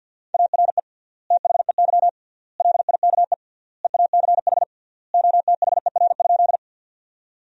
Morse code, heard as a keyed tone, about 50 wpm, 700 Hz. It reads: ARE THEY LIKE EACH OTHER?
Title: 3x Word-Spacing - Sentences from Top 200 Words - 50wpm